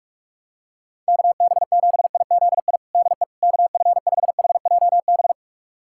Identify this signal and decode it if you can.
Morse code, heard as a keyed tone, about 46 wpm, 690 Hz. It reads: KB7IZI DE KF5HJB
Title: USA Callsign DE Callsign - 6 Characters - 45wpm